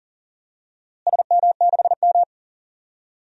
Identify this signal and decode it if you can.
Morse code, heard as a keyed tone, about 40 wpm, 700 Hz. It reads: SM6M
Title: DXCC-100 Call Signs - 4 Characters - Repeated 2x - 40wpm